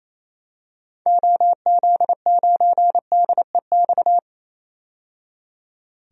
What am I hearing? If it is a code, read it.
Morse code, heard as a keyed tone, about 28 wpm, 700 Hz. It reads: OZ9DEX
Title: DXCC-100 Call Signs - 6 Characters - 28wpm